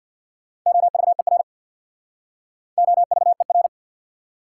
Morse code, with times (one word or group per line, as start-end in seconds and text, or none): 0.7-1.4 OVER
2.8-3.7 OVER